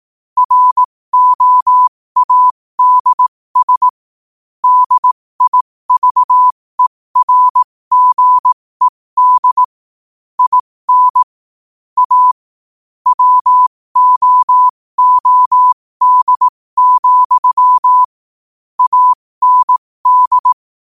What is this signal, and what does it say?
Morse code, heard as a keyed tone, about 18 wpm, spaced as for 15 wpm, 1000 Hz. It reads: ROADS DIVERGED IN A WOOD, AND
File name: qotdfiles / 15wpm